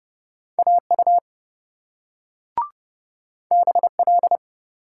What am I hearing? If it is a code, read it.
Morse code, heard as a keyed tone, about 27 wpm, 700 Hz. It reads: AU BL